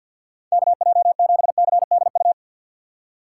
Morse code, heard as a keyed tone, about 53 wpm, 700 Hz. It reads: KJ7CDU
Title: USA Call Signs - 6 Characters - 4x Character Spacing - 50wpm